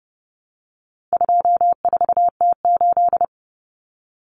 Morse code, heard as a keyed tone, about 30 wpm, 700 Hz. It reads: 24T8
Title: Common Cut Numbers - 4-Digit 30wpm